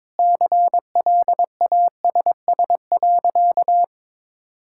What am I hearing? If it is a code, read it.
Morse code, heard as a keyed tone, about 22 wpm, 700 Hz. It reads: CLASS.
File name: practice / 22wpm / Sentences with 8-Word Phrases - 22wpm